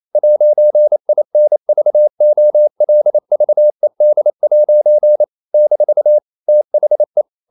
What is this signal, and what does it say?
Morse code, heard as a keyed tone, about 28 wpm, 600 Hz. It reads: 'INVOLVED' - THE
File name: CW / Practice / EnglishQuotations0128